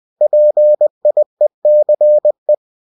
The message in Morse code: PIECE

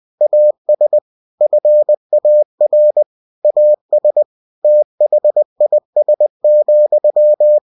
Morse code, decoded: AS FAR AS THIS,

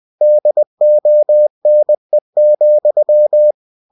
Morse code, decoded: DONE,

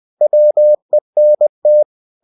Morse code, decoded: WENT

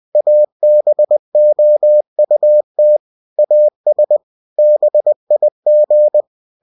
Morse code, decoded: ABOUT AS BIG

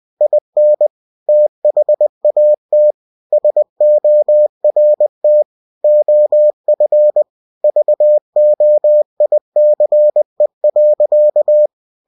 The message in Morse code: IN THAT SORT OF VOICE.

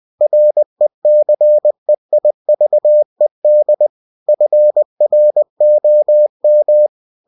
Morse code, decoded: RECEIVED FROM

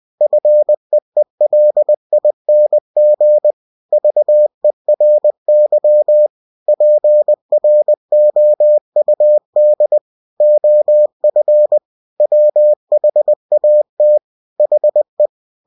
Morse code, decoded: FEELING VERY PROUD OF WHAT HE

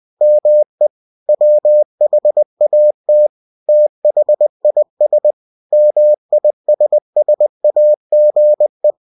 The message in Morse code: ME WHAT THIS MISSAGE

